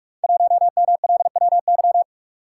Morse code, heard as a keyed tone, about 44 wpm, 690 Hz. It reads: 1MLWY
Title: Code Groups - 5 Letter:Numbers - 45wpm